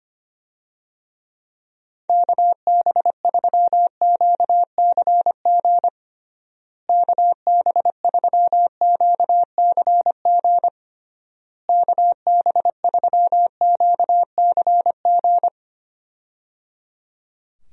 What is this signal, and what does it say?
Morse code, heard as a keyed tone, about 25 wpm, 700 Hz. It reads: KB3QCG KB3QCG KB3QCG
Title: USA Call Signs - 6 Characters - Repeated 3x - 25wpm